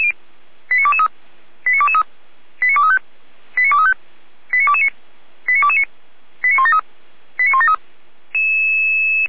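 Five-tone selective call (ZVEI), heard as quick runs of five tones, tones of about 70 ms. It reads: 89223 89223 89235 89235 89229 89229 89173 89173
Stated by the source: Five-tone ZVEI selective call received with an SDR.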